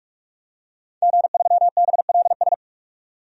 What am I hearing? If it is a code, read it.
Morse code, heard as a keyed tone, about 46 wpm, 700 Hz. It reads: G3BLS
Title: DXCC-100 Call Signs - 5 Characters - 45wpm